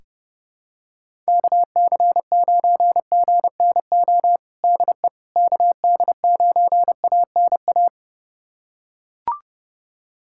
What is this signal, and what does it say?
Morse code, heard as a keyed tone, about 30 wpm, 700 Hz. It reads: KC9GNO DE KD9ANA
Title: USA Callsign DE Callsign - 6 Characters - 30wpm